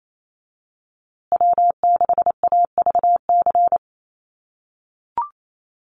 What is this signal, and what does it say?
Morse code, heard as a keyed tone, about 28 wpm, 700 Hz. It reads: W6AVC
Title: USA Call Signs Repeated 3x 28wpm